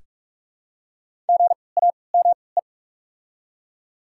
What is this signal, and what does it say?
Morse code, heard as a keyed tone, about 43 wpm, 700 Hz. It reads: G A M E
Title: 3x Character Spacing - Top 500 Words - 45wpm